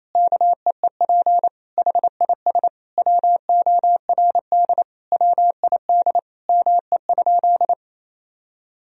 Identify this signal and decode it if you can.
Morse code, heard as a keyed tone, about 28 wpm, 700 Hz. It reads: KEEP HIS WORD WID ME?
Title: Sentences from Top 300 Words and Abbreviations 28wpm